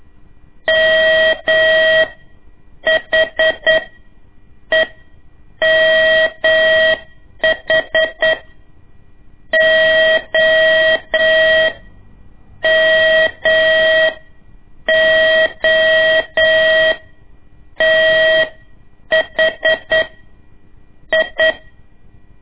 Morse code, heard as a keyed tone, about 9 wpm, spaced as for 7 wpm, 640 Hz. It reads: MHEMH O MOTH I
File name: ik1weg2 / CORSI CW / lez2-1